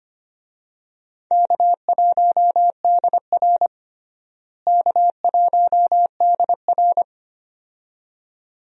Morse code, heard as a keyed tone, about 25 wpm, 700 Hz. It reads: K1DR K1DR
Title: USA Call Signs - 4 Characters - Repeated 2x - 25wpm